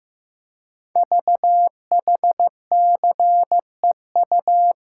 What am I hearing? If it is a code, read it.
Morse code, heard as a keyed tone, about 15 wpm, 690 Hz. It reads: VHCEU
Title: Code Groups - 5 Letters - 15wpm